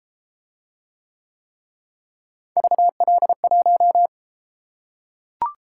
Morse code, heard as a keyed tone, about 32 wpm, 700 Hz. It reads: VL1